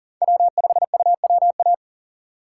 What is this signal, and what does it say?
Morse code, heard as a keyed tone, about 40 wpm, 700 Hz. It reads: W5UWA
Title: USA Call Signs - 5 Characters - 2x Character Spacing - 40wpm